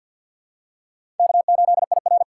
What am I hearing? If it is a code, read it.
Morse code, heard as a keyed tone, about 53 wpm, 700 Hz. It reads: K8IR
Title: USA Call Signs - 4 Characters - 2x Character Spacing - 50wpm